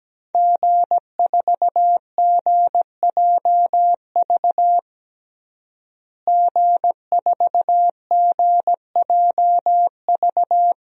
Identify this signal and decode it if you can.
Morse code, heard as a keyed tone, about 17 wpm, 700 Hz. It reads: G4GJV G4GJV
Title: DXCC-100 Call Signs - 5 Characters - Repeated 3x - 17wpm